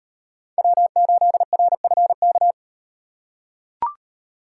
Morse code, heard as a keyed tone, about 38 wpm, 700 Hz. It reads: W8RFK